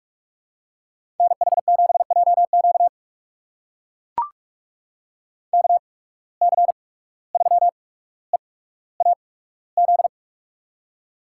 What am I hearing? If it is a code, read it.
Morse code, heard as a keyed tone, about 44 wpm, spaced as for 18 wpm, 700 Hz. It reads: NH7JQ KC3EAZ
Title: USA Call Signs - 8x Character Spacing - 45wpm